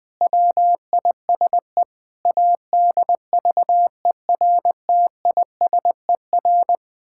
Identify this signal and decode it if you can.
Morse code, heard as a keyed tone, about 20 wpm, 700 Hz. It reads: WISE ADVERTISER
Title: Pangram Challenge v2 - 20wpm